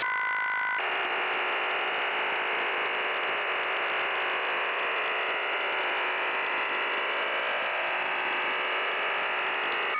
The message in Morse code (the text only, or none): none